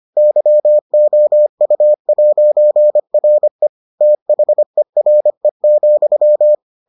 Morse code, heard as a keyed tone, about 25 wpm, 600 Hz. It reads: YOU'RE THERE,